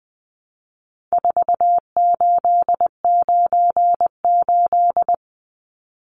Morse code, heard as a keyed tone, about 20 wpm, 700 Hz. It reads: 4898
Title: Numbers - 4-Digit 20wpm